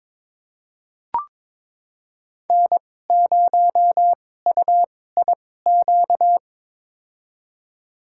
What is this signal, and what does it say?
Morse code, heard as a keyed tone, about 22 wpm, spaced as for 16 wpm, 700 Hz. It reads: N0UIQ
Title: USA Call Signs - 5 Characters - 2x Character Spacing - 22wpm